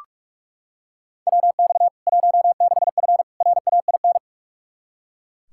Morse code, heard as a keyed tone, about 42 wpm, 700 Hz. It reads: WX 16F RAIN
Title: K1USN SST QSO Practice 45wpm